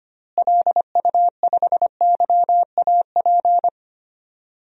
Morse code, heard as a keyed tone, about 25 wpm, 700 Hz. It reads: LU5YAP
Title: DXCC-100 Call Signs - 6 Characters - Repeated 3x - 25wpm